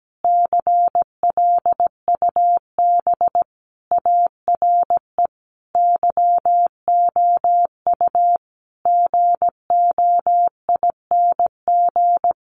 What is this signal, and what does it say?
Morse code, heard as a keyed tone, about 17 wpm, 700 Hz. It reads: CLUB ARE YOU GOING
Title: Sentences from Top 1200 Words 17wpm